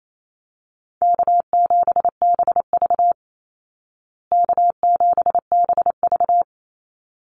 Morse code, heard as a keyed tone, about 28 wpm, 700 Hz. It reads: K7BV K7BV
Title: Easy Call Signs Repeated 2x - 28wpm